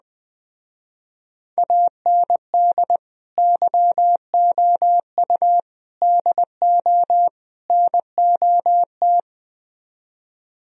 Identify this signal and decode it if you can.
Morse code, heard as a keyed tone, about 20 wpm, 700 Hz. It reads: AND YOU DO NOT
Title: Speed-Racing - Sets of 4 Words 17wpm